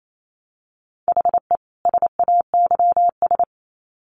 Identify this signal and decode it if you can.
Morse code, heard as a keyed tone, about 28 wpm, 700 Hz. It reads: HE SAYS
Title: Sets of 2 Words 28wpm